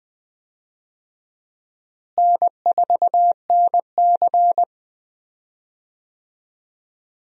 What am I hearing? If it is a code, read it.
Morse code, heard as a keyed tone, about 19 wpm, 700 Hz. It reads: N4NC
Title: USA Call Signs - 4 Characters - Repeated 3x - 20wpm